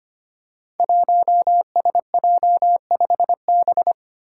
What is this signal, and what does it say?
Morse code, heard as a keyed tone, about 25 wpm, 700 Hz. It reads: 1SJ5B
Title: Five Challenging Letter-Number Combinations 25wpm